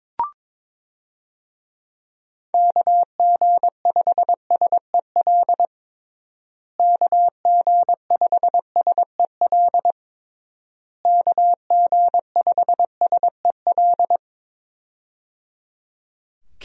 Morse code, heard as a keyed tone, about 22 wpm, 700 Hz. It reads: KG5SEL KG5SEL KG5SEL
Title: USA Call Signs - 6 Characters - Repeated 3x - 22wpm